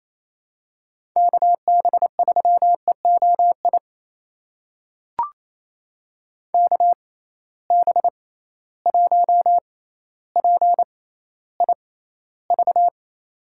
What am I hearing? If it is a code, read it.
Morse code, heard as a keyed tone, about 28 wpm, spaced as for 14 wpm, 700 Hz. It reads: KB3EOI KB1PIV
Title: USA Call Signs - 6 Characters - 6x Character Spacing - 28wpm